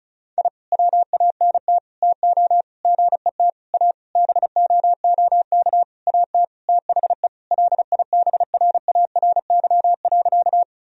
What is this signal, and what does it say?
Morse code, heard as a keyed tone, about 36 wpm, 700 Hz. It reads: I WANT TO GET A BOOK AT THE LIBRARY.
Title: Sentences from Top 1000 Words - Review Part 1 - 35wpm